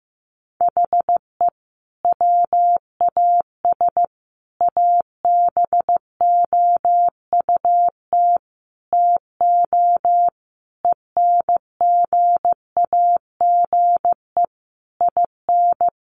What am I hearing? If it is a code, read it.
Morse code, heard as a keyed tone, about 15 wpm, 700 Hz. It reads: HE WAS ABOUT TO ENGAGE IN
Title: Sentences from Top 1500 Words 15wpm